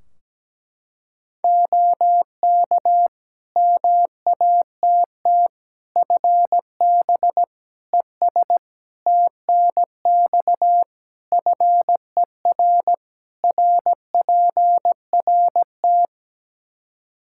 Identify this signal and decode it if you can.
Morse code, heard as a keyed tone, about 17 wpm, 700 Hz. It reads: OK MATT FB ES TNX FER RPRT
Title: Practice QSO - Exchange 2 - Info - 17wpm